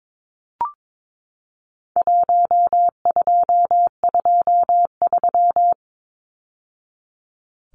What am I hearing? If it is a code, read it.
Morse code, heard as a keyed tone, about 22 wpm, 700 Hz. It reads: 1223
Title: Numbers - Contest Style - 4-Digit 22wpm